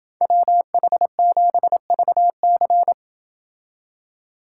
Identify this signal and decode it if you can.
Morse code, heard as a keyed tone, about 30 wpm, 700 Hz. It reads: WH7VC